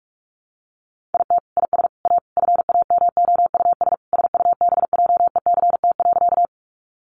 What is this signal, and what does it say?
Morse code, heard as a keyed tone, about 44 wpm, 700 Hz. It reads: IT IS A FAMOUS SUBJECT.
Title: Sentences from Top 1500 Words 45wpm